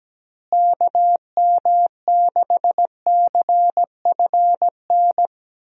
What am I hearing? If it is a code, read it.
Morse code, heard as a keyed tone, about 17 wpm, 700 Hz. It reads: KM6CFN
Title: USA Call Signs - 6 Characters - Repeated 2x - 17wpm